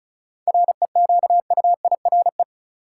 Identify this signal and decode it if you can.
Morse code, heard as a keyed tone, about 34 wpm, 700 Hz. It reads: REQUIRE
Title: Seven Letter Words - 35wpm